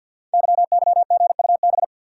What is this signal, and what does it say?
Morse code, heard as a keyed tone, about 52 wpm, 700 Hz. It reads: YYGUB
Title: Code Groups - 5 Letters - 50wpm